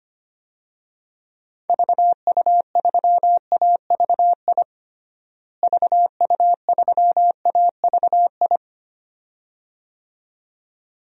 Morse code, heard as a keyed tone, about 25 wpm, 700 Hz. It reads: VU3AVI VU3AVI
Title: DXCC-100 Call Signs - 6 Characters - Repeated 2x - 25wpm